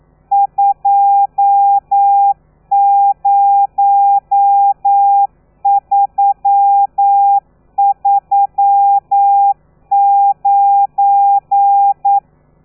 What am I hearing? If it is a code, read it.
Morse code, heard as a keyed tone, about 9 wpm, 800 Hz. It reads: 20339
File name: telegrafia / i2tfj / Lez12 40